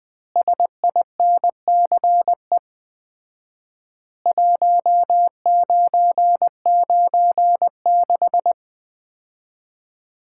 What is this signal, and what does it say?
Morse code, heard as a keyed tone, about 20 wpm, 700 Hz. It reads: SINCE 1996
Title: Common QSO Phrases - 4x Word-Spacing - 20wpm